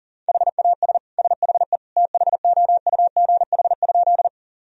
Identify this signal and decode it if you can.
Morse code, heard as a keyed tone, about 37 wpm, 700 Hz. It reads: HAS SHE THOUGH?
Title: Sentences from Top 1000 Words - Review Part 2 - 40wpm